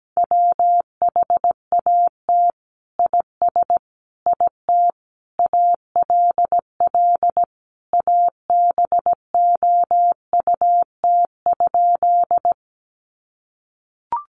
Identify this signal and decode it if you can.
Morse code, heard as a keyed tone, about 17 wpm, 700 Hz. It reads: WHAT IS IT ALL ABOUT?
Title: Sentences from Top 100 Words 17wpm